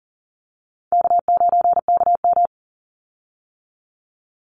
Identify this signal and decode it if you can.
Morse code, heard as a keyed tone, about 38 wpm, 700 Hz. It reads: K9KM
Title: Call Signs - 40wpm